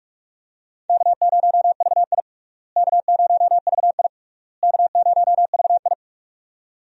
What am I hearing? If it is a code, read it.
Morse code, heard as a keyed tone, about 44 wpm, 700 Hz. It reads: K0VI K0VI K0VI
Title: USA Call Signs - 4 Characters - Repeated 3x - 45wpm